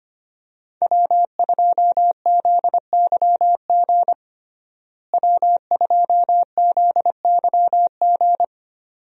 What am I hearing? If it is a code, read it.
Morse code, heard as a keyed tone, about 25 wpm, 700 Hz. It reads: W2ZYG W2ZYG
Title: USA Call Signs - 5 Characters - Repeated 2x - 25wpm